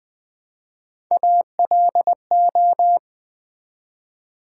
Morse code, heard as a keyed tone, about 20 wpm, 700 Hz. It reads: ALO
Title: Build A Word - 1 Letter At A Time - 3 to 5-Letter Words - 20wpm